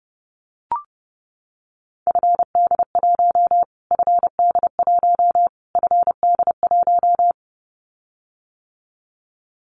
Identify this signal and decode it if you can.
Morse code, heard as a keyed tone, about 30 wpm, 700 Hz. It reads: FD1 FD1 FD1
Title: Callsign Prefixes for Top 25 DXCC - Repeated 3x - 30wpm